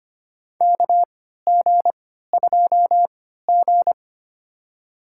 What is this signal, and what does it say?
Morse code, heard as a keyed tone, about 25 wpm, spaced as for 14 wpm, 700 Hz. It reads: KG2G
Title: USA Call Signs - 4 Characters - 3x Character Spacing - 25wpm